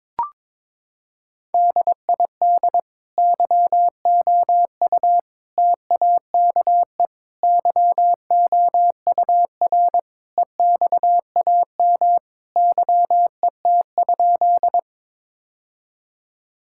Morse code, heard as a keyed tone, about 22 wpm, 700 Hz. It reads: DID YOU TAKE YOUR EXAM YET?
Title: Sentences from Top 1000 Words - Review Part 5 - 22wpm